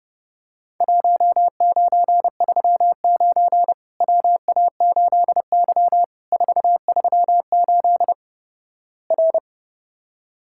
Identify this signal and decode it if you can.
Morse code, heard as a keyed tone, about 28 wpm, 700 Hz. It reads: 1939 WA8Y 438 R
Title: Worked All Europe DX Contest - QTC Exchange - 30wpm